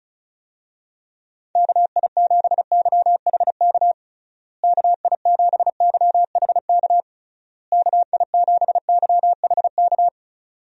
Morse code, heard as a keyed tone, about 34 wpm, 700 Hz. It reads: KI7YHK KI7YHK KI7YHK
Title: USA Call Signs - 6 Characters - Repeated 3x - 35wpm